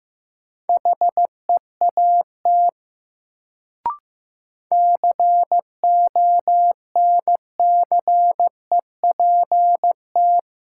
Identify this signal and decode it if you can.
Morse code, heard as a keyed tone, about 15 wpm, 700 Hz. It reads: HEAT CONCEPT
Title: Sentences from Top 1100 Words - New Word Drill Alt - 15wpm